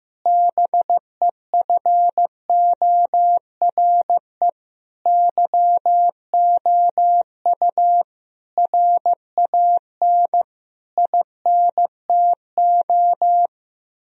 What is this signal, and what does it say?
Morse code, heard as a keyed tone, about 15 wpm, 700 Hz. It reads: BEFORE YOU RAN INTO
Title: Sentences from Top 1000 Words - Review Part 3 - 15wpm